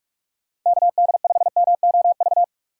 Morse code, heard as a keyed tone, about 46 wpm, 700 Hz. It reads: KD5MOV
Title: USA Call Signs - 6 Characters - 4x Character Spacing - 45wpm